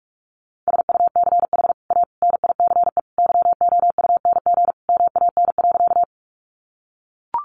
Morse code, heard as a keyed tone, about 44 wpm, 700 Hz. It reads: SUCH A NICE YOUNG MAN.
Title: Sentences from Top 1100 Words 45wpm